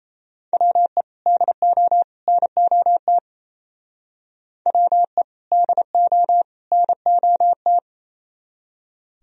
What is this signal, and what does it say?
Morse code, heard as a keyed tone, about 35 wpm, 700 Hz. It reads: WE DO NOT WE DO NOT